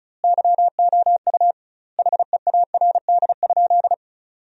Morse code, heard as a keyed tone, about 34 wpm, 700 Hz. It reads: YOU HEARD?
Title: 2x Word-Spacing - Sentences from Top 400 Words - 35wpm